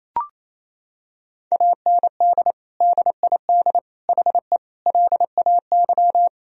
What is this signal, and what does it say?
Morse code, heard as a keyed tone, about 27 wpm, 700 Hz. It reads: AND DID HE LAY
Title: Sentences from Top 1000 Words - Review Part 2 - 28wpm